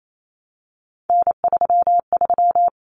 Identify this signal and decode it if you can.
Morse code, heard as a keyed tone, about 28 wpm, 700 Hz. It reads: N33
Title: Common Cut Numbers - 3-Digit 28wpm